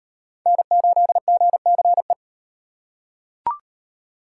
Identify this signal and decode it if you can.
Morse code, heard as a keyed tone, about 38 wpm, 700 Hz. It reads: N8GCE